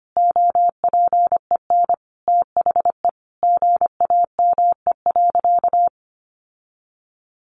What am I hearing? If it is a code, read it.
Morse code, heard as a keyed tone, about 25 wpm, 700 Hz. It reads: OPEN THE GAME.
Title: Sentences from Top 1500 Words 25wpm